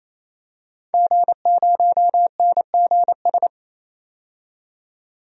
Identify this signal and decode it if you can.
Morse code, heard as a keyed tone, about 28 wpm, 700 Hz. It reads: G0NGS
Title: DXCC-100 Call Signs - 5 Characters - 28wpm